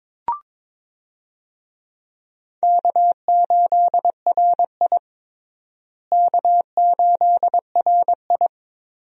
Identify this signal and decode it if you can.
Morse code, heard as a keyed tone, about 21 wpm, 700 Hz. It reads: K8RI K8RI
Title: USA Call Signs - 4 Characters - Repeated 3x - 22wpm